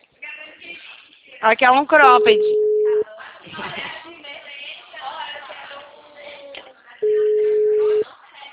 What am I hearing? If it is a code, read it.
Ringback tone (Europe, 425 Hz), with speech.
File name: out- -2009- - - .2330